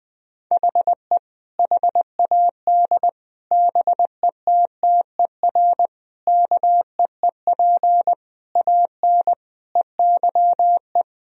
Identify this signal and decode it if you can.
Morse code, heard as a keyed tone, about 21 wpm, 700 Hz. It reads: HE HAD BETTER KEEP AN EYE
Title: Sentences from Top 1000 Words - Review Part 1 - 20wpm